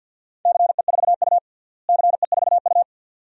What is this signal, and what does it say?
Morse code, heard as a keyed tone, about 47 wpm, 700 Hz. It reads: CE4U CE4U
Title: DXCC-100 Call Signs - 4 Characters - Repeated 2x - 50wpm